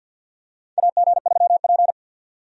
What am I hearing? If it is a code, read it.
Morse code, heard as a keyed tone, about 49 wpm, 700 Hz. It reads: AG3P
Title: USA Call Signs - 4 Characters - 50wpm